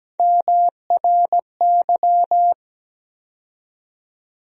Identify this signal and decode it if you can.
Morse code, heard as a keyed tone, about 17 wpm, 700 Hz. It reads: MRY
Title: USA Call Sign Postfixes - 3 Letters 17wpm